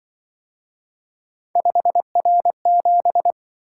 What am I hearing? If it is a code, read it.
Morse code, heard as a keyed tone, about 25 wpm, 700 Hz. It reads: 5R7
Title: Speed-Racing - Three Letter-Number Combinations - 20wpm